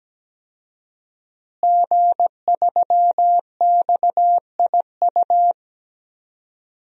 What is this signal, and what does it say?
Morse code, heard as a keyed tone, about 17 wpm, 700 Hz. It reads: G3XIU
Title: DXCC-100 Call Signs - 5 Characters - Repeated 2x - 17wpm